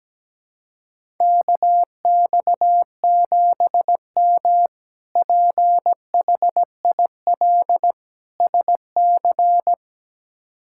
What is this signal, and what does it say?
Morse code, heard as a keyed tone, about 17 wpm, 700 Hz. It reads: KX7M PHIL SC
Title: K1USN SST QSO Practice - 17wpm